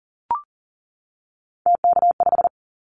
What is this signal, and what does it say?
Morse code, heard as a keyed tone, about 42 wpm, 700 Hz. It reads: TK5
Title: Callsign Prefixes for Top 100 DXCC - 40wpm